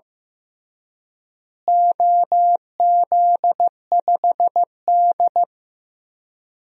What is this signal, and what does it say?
Morse code, heard as a keyed tone, about 15 wpm, 700 Hz. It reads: OZ5D
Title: DXCC-100 Call Signs - 4 Characters - Repeated 2x - 15wpm